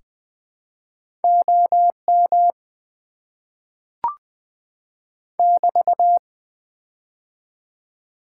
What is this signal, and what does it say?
Morse code, heard as a keyed tone, about 20 wpm, 690 Hz. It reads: OM <BT>
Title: Top 90 QSO Elements - 3x Character Spacing - 20wpm